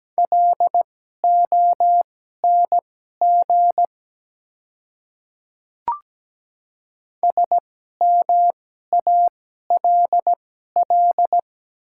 Morse code, heard as a keyed tone, about 17 wpm, spaced as for 12 wpm, 700 Hz. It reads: LONG SMALL
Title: 2x Character Spacing - Top 300 Words - 17wpm